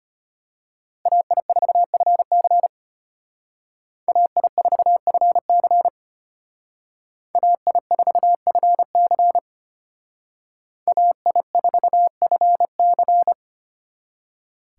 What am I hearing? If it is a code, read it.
Morse code, heard as a keyed tone, about 33 wpm, 700 Hz. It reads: AI4FC AI4FC AI4FC AI4FC